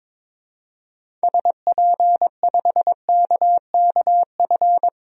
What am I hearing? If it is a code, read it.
Morse code, heard as a keyed tone, about 22 wpm, 700 Hz. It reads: SP5KKF
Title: DXCC-100 Call Signs - 6 Characters - 22wpm